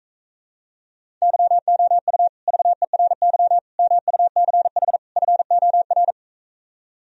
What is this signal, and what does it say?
Morse code, heard as a keyed tone, about 45 wpm, 700 Hz. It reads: YOU VERY MUCH FOR